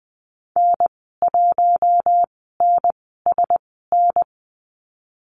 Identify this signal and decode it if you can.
Morse code, heard as a keyed tone, about 20 wpm, spaced as for 14 wpm, 700 Hz. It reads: N1NSN
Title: 076 Lesson 13g - Letter L - Call Signs 20wpm